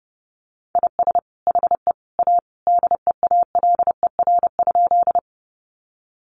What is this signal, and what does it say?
Morse code, heard as a keyed tone, about 30 wpm, 700 Hz. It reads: IS HE A DEALER?
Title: Sentences from Top 1100 Words 30wpm